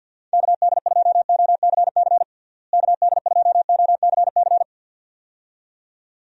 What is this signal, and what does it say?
Morse code, heard as a keyed tone, about 50 wpm, 700 Hz. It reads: KD2OCC KD2OCC
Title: USA Call Signs - 6 Characters - Repeated 2x - 50wpm